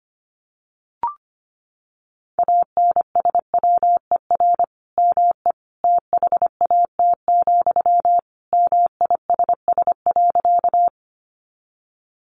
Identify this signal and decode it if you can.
Morse code, heard as a keyed tone, about 25 wpm, 700 Hz. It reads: ANSWER ME THAT, MISS.
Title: Sentences from Top 500 Words 25wpm